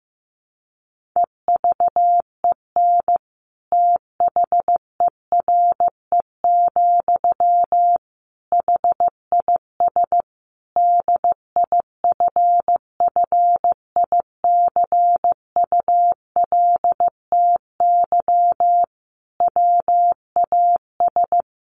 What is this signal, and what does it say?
Morse code, heard as a keyed tone, about 15 wpm, 700 Hz. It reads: EVEN THERE, HIS DIFFICULTY WAS
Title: Sentences from Top 1500 Words 15wpm